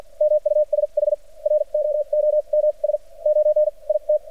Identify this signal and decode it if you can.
Morse code, heard as a keyed tone, about 47 wpm, spaced as for 36 wpm, 600 Hz. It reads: MUSHROOMS 9 IT
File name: archived / CRSnet